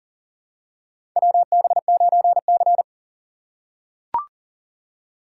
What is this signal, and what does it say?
Morse code, heard as a keyed tone, about 40 wpm, 700 Hz. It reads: WB9C
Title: USA Call Signs - 4 Characters - 6x Character Spacing - 40wpm